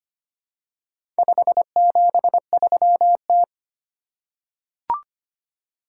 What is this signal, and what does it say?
Morse code, heard as a keyed tone, about 25 wpm, 700 Hz. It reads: 573T
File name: practice / 25wpm / Four Letter-Number Combinations - 25wpm